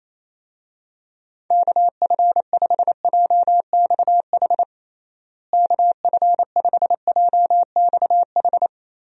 Morse code, heard as a keyed tone, about 28 wpm, 700 Hz. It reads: KF5JXH KF5JXH
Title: USA Call Signs - 6 Characters - Repeated 2x - 28wpm